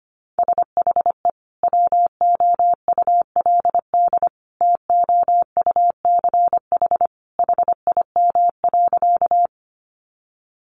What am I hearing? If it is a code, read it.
Morse code, heard as a keyed tone, about 25 wpm, 700 Hz. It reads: SHE WOULD TOUCH HIM.
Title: Sentences from Top 1200 Words 25wpm